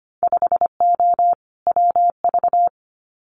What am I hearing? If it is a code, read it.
Morse code, heard as a keyed tone, about 25 wpm, 700 Hz. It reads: 5O WV
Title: Winter Field Day Exchanges Encore 25wpm